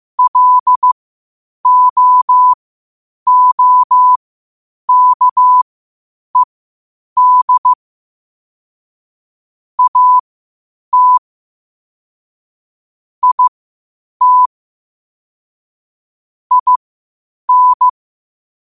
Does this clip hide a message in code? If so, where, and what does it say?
Yes: Morse code from 0.2 to 17.9 s, heard as a keyed tone, about 15 wpm, spaced as for 8 wpm, 1000 Hz. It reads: LOOKED AT IT IN